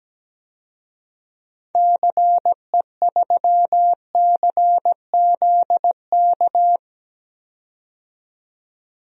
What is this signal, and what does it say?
Morse code, heard as a keyed tone, about 17 wpm, 700 Hz. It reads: CE3CZK
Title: DXCC-100 Call Signs - 6 Characters - 17wpm